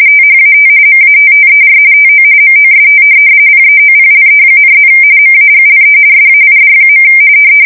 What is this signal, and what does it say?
RTTY, about 45.45 baud. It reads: THAT MATTER TO MOST PEOPLE,+ MAGLE SAID.
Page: 170 Shift 2125/2295